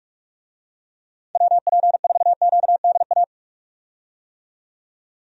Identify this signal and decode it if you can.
Morse code, heard as a keyed tone, about 44 wpm, 700 Hz. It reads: WP4QDA
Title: USA Call Signs - 6 Characters - 45wpm